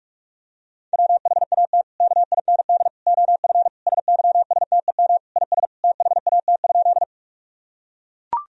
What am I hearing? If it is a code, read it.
Morse code, heard as a keyed tone, about 42 wpm, 700 Hz. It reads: WHAT KIND OF SYSTEM IS THAT?
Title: Sentences from Top 1000 Words - Review Part 3 - 45wpm